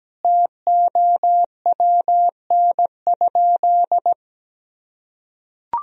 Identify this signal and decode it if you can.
Morse code, heard as a keyed tone, about 17 wpm, 700 Hz. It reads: TOWN?
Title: Sentences from Top 1000 Words - Review Part 4 - 17wpm